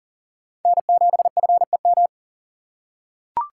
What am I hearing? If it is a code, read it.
Morse code, heard as a keyed tone, about 40 wpm, 700 Hz. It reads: N7FEM
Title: USA Call Signs - 5 Characters - 2x Character Spacing - 40wpm